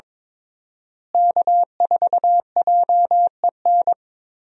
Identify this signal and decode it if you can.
Morse code, heard as a keyed tone, about 23 wpm, 700 Hz. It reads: K4JEN
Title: USA Call Signs - 5 Characters - Repeated 3x - 22wpm